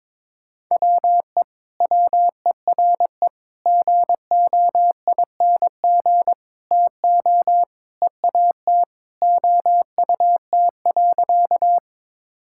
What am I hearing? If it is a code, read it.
Morse code, heard as a keyed tone, about 21 wpm, 700 Hz. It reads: WE WERE GOING TO EAT OUT.
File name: practice / 22wpm / Sentences with 4-Word Phrases - 22wpm